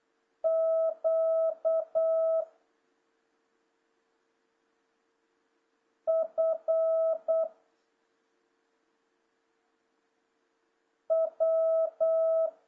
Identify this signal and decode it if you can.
Morse code, heard as a keyed tone, about 8 wpm, 650 Hz. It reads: Q F W
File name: jh8cbh / 2026cw